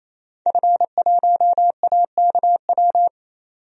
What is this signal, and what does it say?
Morse code, heard as a keyed tone, about 28 wpm, 700 Hz. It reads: F1AKW
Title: DXCC-100 Call Signs - 5 Characters - 28wpm